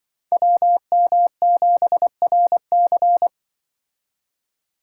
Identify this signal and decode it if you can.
Morse code, heard as a keyed tone, about 25 wpm, 700 Hz. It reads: WM7RC